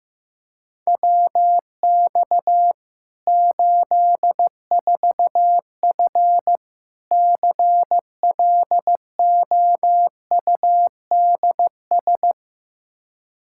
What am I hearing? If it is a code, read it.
Morse code, heard as a keyed tone, about 15 wpm, 700 Hz. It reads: WX 84F CLOUDS
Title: Practice QSO - Exchange 2 - Info - 15wpm